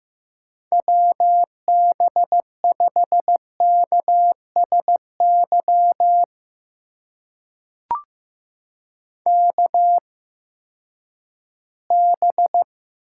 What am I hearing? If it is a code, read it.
Morse code, heard as a keyed tone, about 15 wpm, spaced as for 10 wpm, 700 Hz. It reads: WB5KSY KB
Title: USA Call Signs - 6 Characters - 8x Character Spacing - 15wpm